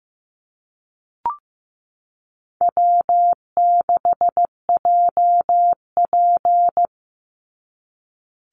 Morse code, heard as a keyed tone, about 15 wpm, 700 Hz. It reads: W6JP
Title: NA SOTA Chaser Callsigns 15wpm